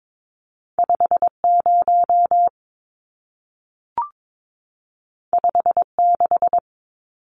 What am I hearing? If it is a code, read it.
Morse code, heard as a keyed tone, about 22 wpm, 700 Hz. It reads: 50 56
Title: Numbers - 2-Digit 22wpm